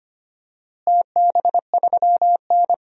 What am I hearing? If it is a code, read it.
Morse code, heard as a keyed tone, about 25 wpm, 700 Hz. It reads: TB3N
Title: DXCC-100 Call Signs - 4 Characters - Repeated 2x - 25wpm